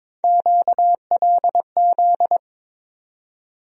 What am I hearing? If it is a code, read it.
Morse code, heard as a keyed tone, about 22 wpm, 700 Hz. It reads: QLZ
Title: USA Call Sign Postfixes - 3 Letters 22wpm